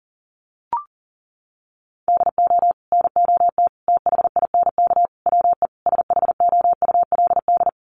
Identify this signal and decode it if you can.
Morse code, heard as a keyed tone, about 40 wpm, 700 Hz. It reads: DO NOT THINK WE SHOULD
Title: Sets of 5 Words Encore 40wpm